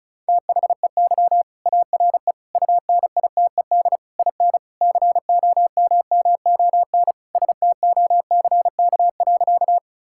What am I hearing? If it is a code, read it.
Morse code, heard as a keyed tone, about 36 wpm, 700 Hz. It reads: THEY ARE UNITED IN COMMON STOCK.
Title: Sentences from Top 1000 Words - Review Part 5 - 35wpm